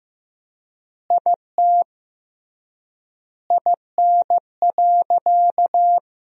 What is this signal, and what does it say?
Morse code, heard as a keyed tone, about 15 wpm, 700 Hz. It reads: IT IN.
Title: 3x Word-Spacing - Sentences from Top 200 Words - 15wpm